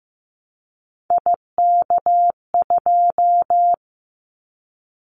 Morse code, heard as a keyed tone, about 15 wpm, 700 Hz. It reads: IK2
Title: Callsign Prefixes for Top 50 DXCC - 15wpm